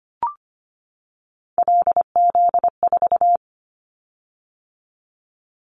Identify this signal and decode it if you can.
Morse code, heard as a keyed tone, about 25 wpm, 700 Hz. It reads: LZ4
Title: Callsign Prefixes for Top 100 DXCC - 25wpm